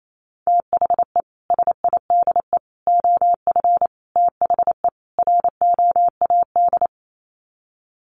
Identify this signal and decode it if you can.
Morse code, heard as a keyed tone, about 28 wpm, 700 Hz. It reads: THE SIDE OF THE ROAD
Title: Sets of 5 Words 28wpm